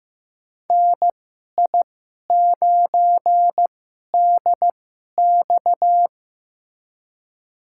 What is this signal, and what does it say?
Morse code, heard as a keyed tone, about 15 wpm, spaced as for 11 wpm, 700 Hz. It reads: NI9DX
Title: USA Call Signs - 5 Characters - 2x Character Spacing - 15wpm